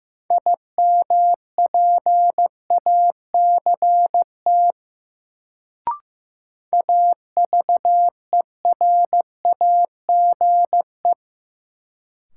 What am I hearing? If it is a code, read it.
Morse code, heard as a keyed tone, about 15 wpm, 700 Hz. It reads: IMPACT AVERAGE
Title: Sentences from Top 1000 Words - Review Part 6 - New Word Drill - 15wpm